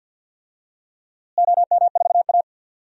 Morse code, heard as a keyed tone, about 52 wpm, 700 Hz. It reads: OM4A
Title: DXCC-100 Call Signs - 4 Characters - Repeated 2x - 50wpm